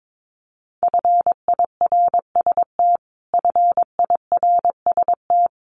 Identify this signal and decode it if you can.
Morse code, heard as a keyed tone, about 22 wpm, 700 Hz. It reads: FIRST FIRST
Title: Top 500 Words Repeated 22wpm